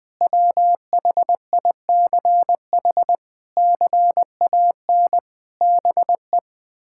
Morse code, heard as a keyed tone, about 21 wpm, 700 Hz. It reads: WHICH CAN BE
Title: Sentences from Top 1600 Words 20wpm